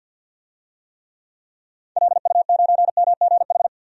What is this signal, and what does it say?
Morse code, heard as a keyed tone, about 50 wpm, 700 Hz. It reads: LU9MGH
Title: DXCC-100 Call Signs - 6 Characters - 50wpm